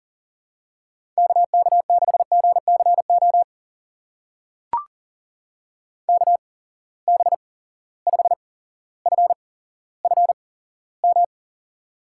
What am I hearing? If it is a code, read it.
Morse code, heard as a keyed tone, about 41 wpm, spaced as for 17 wpm, 700 Hz. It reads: KK6GCO KB5FFM
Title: USA Call Signs - 6 Characters - 8x Character Spacing - 40wpm